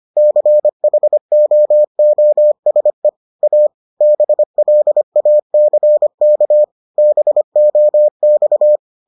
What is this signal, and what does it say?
Morse code, heard as a keyed tone, about 25 wpm, 600 Hz. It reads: CHOOSE A BLACK BOX